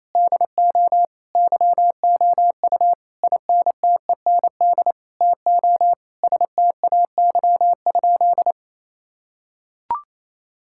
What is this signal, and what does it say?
Morse code, heard as a keyed tone, about 28 wpm, 700 Hz. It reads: DO YOU INTEND TO STAY?
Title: Sentences from Top 1600 Words 28wpm